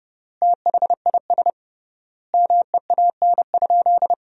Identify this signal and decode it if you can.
Morse code, heard as a keyed tone, about 30 wpm, 700 Hz. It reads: THIS MEAN?
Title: 3x Word-Spacing - Sentences from Top 200 Words - 30wpm